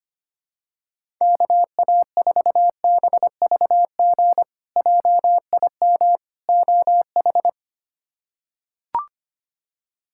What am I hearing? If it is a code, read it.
Morse code, heard as a keyed tone, about 25 wpm, 700 Hz. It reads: KA4BVG JIM OH
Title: K1USN SST QSO Practice - 25wpm